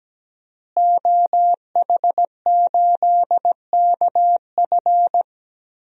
Morse code, heard as a keyed tone, about 17 wpm, 700 Hz. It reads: OH8KF
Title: DXCC-100 Call Signs - 5 Characters - Repeated 3x - 17wpm